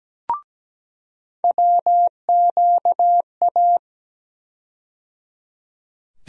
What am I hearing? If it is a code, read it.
Morse code, heard as a keyed tone, about 17 wpm, 700 Hz. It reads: WQA
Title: USA Call Sign Postfixes - 3 Letters 17wpm